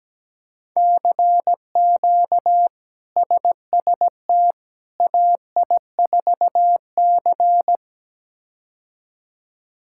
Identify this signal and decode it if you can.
Morse code, heard as a keyed tone, about 17 wpm, 700 Hz. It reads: CQ SST AI4C
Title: K1USN SST QSO Practice - 17wpm